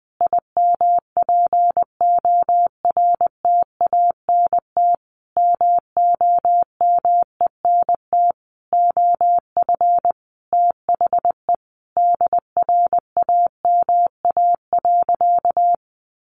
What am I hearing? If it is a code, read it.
Morse code, heard as a keyed tone, about 20 wpm, 700 Hz. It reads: IMPORTANT MOMENT OF THE DRAMA.
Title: Sentences from Top 1500 Words 20wpm